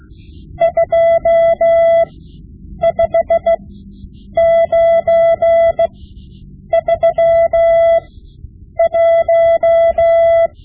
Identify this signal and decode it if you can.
Morse code, heard as a keyed tone, about 16 wpm, spaced as for 8 wpm, 670 Hz. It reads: 25931
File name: Morse Lessons / Morse Lessons / Tape-5B